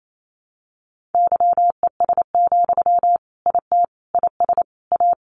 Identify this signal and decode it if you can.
Morse code, heard as a keyed tone, about 28 wpm, 700 Hz. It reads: YES, IT IS A
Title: Sentences from Top 1400 Words 28wpm